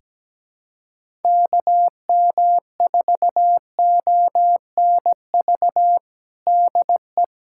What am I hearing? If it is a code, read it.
Morse code, heard as a keyed tone, about 17 wpm, 700 Hz. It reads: KM4ONV DE
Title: USA Callsign DE Callsign - 6 Characters - 17wpm